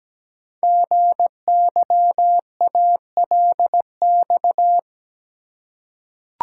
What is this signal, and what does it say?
Morse code, heard as a keyed tone, about 17 wpm, 700 Hz. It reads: GYALX
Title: Code Groups - 5 Letters - 17wpm